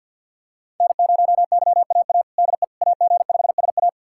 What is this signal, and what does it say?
Morse code, heard as a keyed tone, about 52 wpm, 700 Hz. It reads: N0YAA DE AG5SA
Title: USA Callsign DE Callsign 50wpm